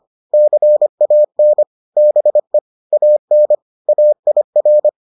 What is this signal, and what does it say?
Morse code, heard as a keyed tone, about 25 wpm, 600 Hz. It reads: CAN BE AN AIR